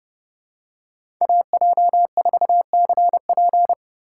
Morse code, heard as a keyed tone, about 30 wpm, 700 Hz. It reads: AJ4CP
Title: USA Call Signs - 5 Characters - 2x Character Spacing - 30wpm